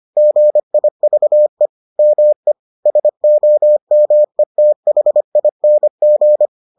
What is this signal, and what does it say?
Morse code, heard as a keyed tone, about 25 wpm, 600 Hz. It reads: GIVE ME SOMETHING